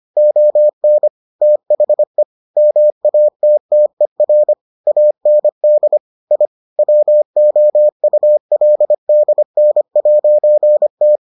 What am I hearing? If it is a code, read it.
Morse code, heard as a keyed tone, about 25 wpm, 600 Hz. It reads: ON THE MATTER AND I WOULDN'T